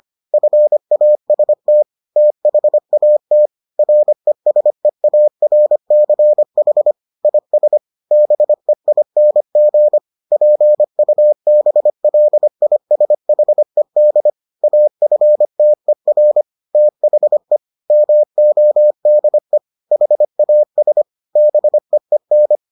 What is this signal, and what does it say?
Morse code, heard as a keyed tone, about 25 wpm, 600 Hz. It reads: FAST THAT RESEARCH IS BEING PUBLISHED AFTER THE MODE HAS BEEN